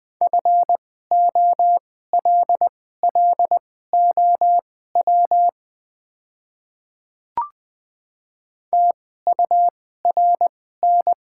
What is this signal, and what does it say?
Morse code, heard as a keyed tone, about 21 wpm, spaced as for 15 wpm, 700 Hz. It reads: FOLLOW TURN
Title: 2x Character Spacing - Top 200 Words - 20wpm